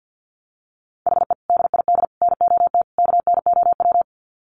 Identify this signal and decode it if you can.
Morse code, heard as a keyed tone, about 50 wpm, 700 Hz. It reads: HE DID NOT KNOW
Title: Sets of 4 Words 50wpm